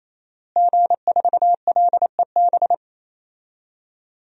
Morse code, heard as a keyed tone, about 28 wpm, 700 Hz. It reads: G4LEB
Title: DXCC-100 Call Signs - 5 Characters - Repeated 3x - 28wpm